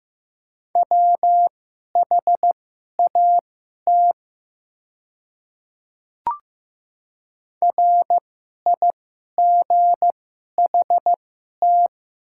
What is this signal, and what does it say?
Morse code, heard as a keyed tone, about 15 wpm, spaced as for 11 wpm, 700 Hz. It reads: WHAT RIGHT
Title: 2x Character Spacing - Top 500 Words - 15wpm